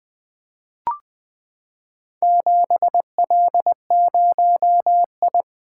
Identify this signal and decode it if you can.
Morse code, heard as a keyed tone, about 20 wpm, 700 Hz. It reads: 7L0I
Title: Four Letter-Number Combinations - 20wpm